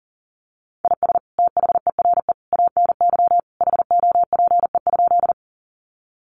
Morse code, heard as a keyed tone, about 40 wpm, 700 Hz. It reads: IS THERE ANY HOPE?
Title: Sentences from Top 1000 Words 40wpm